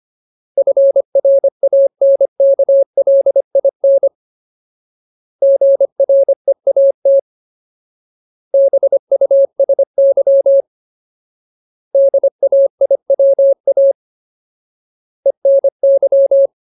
Morse code, heard as a keyed tone, about 25 wpm, 550 Hz. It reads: FRANKLIN GREAT BUSY DAIWA ENY
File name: Ultimate Word List / 25WPM Ultimate